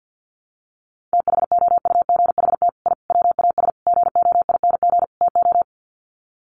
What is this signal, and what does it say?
Morse code, heard as a keyed tone, about 50 wpm, 700 Hz. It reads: THOUGHT I WAS GOING TO
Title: Sets of 5 Words Encore2 50wpm